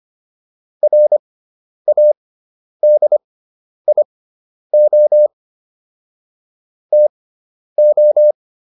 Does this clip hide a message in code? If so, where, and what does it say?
Yes: Morse code from 0.8 to 8.3 s, heard as a keyed tone, about 25 wpm, spaced as for 10 wpm, 600 Hz. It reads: RADIO TO